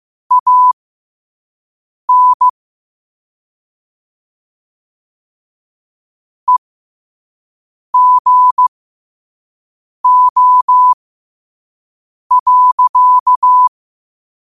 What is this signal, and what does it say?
Morse code, heard as a keyed tone, about 15 wpm, spaced as for 5 wpm, 1000 Hz. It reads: AN EGO.
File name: 5wpm